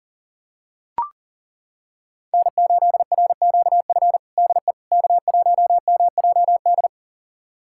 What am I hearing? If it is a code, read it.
Morse code, heard as a keyed tone, about 40 wpm, 700 Hz. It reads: N8RQF DE K1MJD
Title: USA Callsign DE Callsign - 5 Characters - 40wpm